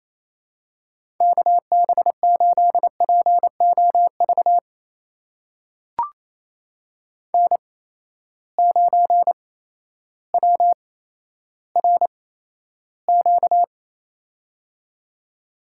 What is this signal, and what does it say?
Morse code, heard as a keyed tone, about 28 wpm, spaced as for 13 wpm, 700 Hz. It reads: KB8POV N9WRQ
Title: USA Call Signs - 8x Character Spacing - 28wpm